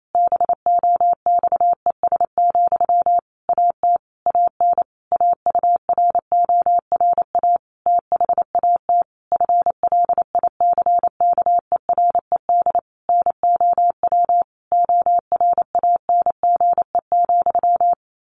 Morse code, heard as a keyed tone, about 28 wpm, 700 Hz. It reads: BOXES, AT AN AURORA THAT FLICKERED NOW ORANGE,
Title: A Sound Of Thunder 28wpm